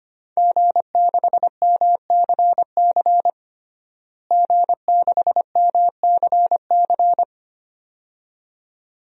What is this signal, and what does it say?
Morse code, heard as a keyed tone, about 25 wpm, 700 Hz. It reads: G6MCC G6MCC
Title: DXCC-100 Call Signs - 5 Characters - Repeated 2x - 25wpm